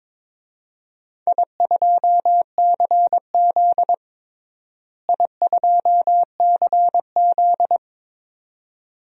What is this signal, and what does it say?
Morse code, heard as a keyed tone, about 21 wpm, 700 Hz. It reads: I2CZ I2CZ
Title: DXCC-100 Call Signs - 4 Characters - Repeated 2x - 22wpm